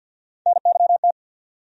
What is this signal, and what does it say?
Morse code, heard as a keyed tone, about 51 wpm, 690 Hz. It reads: NYT
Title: USA Call Sign Postfixes - 3 Letters 50wpm